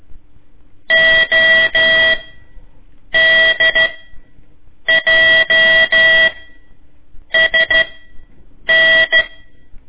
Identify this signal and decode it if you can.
Morse code, heard as a keyed tone, about 13 wpm, spaced as for 7 wpm, 1950 Hz. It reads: ODJSN
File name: ik1weg2 / CORSI CW / lez4-1